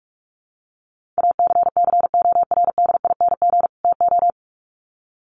Morse code, heard as a keyed tone, about 44 wpm, 700 Hz. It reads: ACCORDING TO
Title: Sets of 2 Words 45wpm